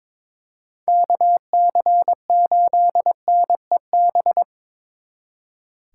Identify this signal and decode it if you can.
Morse code, heard as a keyed tone, about 26 wpm, 700 Hz. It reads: KC8NEB
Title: Speed Racing - USA 2x3 Call Signs 22wpm